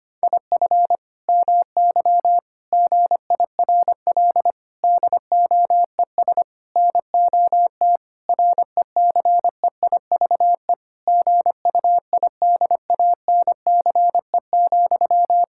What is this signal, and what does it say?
Morse code, heard as a keyed tone, about 25 wpm, 700 Hz. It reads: IF MY GIRL DOES NOT RECEIVE GUIDANCE,
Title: Sentences from Top 1600 Words 25wpm